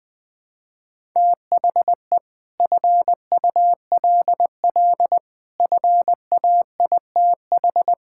Morse code, heard as a keyed tone, about 20 wpm, 700 Hz. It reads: THE FULL FAITH
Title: Sets of 9 Words - 20wpm